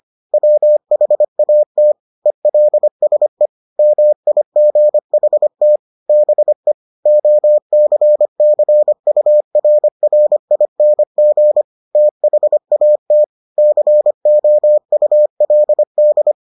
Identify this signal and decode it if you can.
Morse code, heard as a keyed tone, about 25 wpm, 600 Hz. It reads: WHAT ELSE MIGHT BE OCCURRING THAT COULD